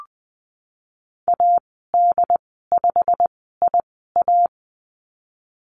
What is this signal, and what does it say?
Morse code, heard as a keyed tone, about 20 wpm, spaced as for 15 wpm, 700 Hz. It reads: AD5IA
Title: 088 Lesson 15g - Number 5 - Call Signs 20wpm